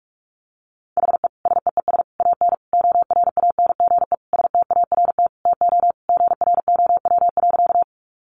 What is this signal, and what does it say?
Morse code, heard as a keyed tone, about 44 wpm, 700 Hz. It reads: HE SEES AN ORANGE START TO GROW.
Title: Easy Phrases 45wpm